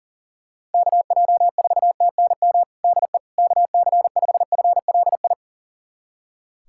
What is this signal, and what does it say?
Morse code, heard as a keyed tone, about 40 wpm, 700 Hz. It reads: KJ4TNM DE KC5FLI
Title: USA Callsign DE Callsign - 6 Characters - 40wpm